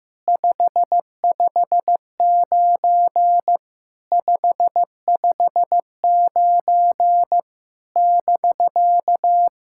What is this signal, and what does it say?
Morse code, heard as a keyed tone, about 15 wpm, 700 Hz. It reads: 559 559 <BK>
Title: SOTA QSO Practice 15wpm